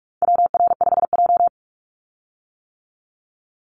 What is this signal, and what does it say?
Morse code, heard as a keyed tone, about 44 wpm, 700 Hz. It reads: WR5J
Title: Call Signs - 45wpm